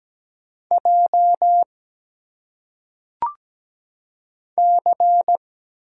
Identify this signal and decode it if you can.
Morse code, heard as a keyed tone, about 17 wpm, 700 Hz. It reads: J C
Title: Single Letters - Phonetic Pronunciation 17wpm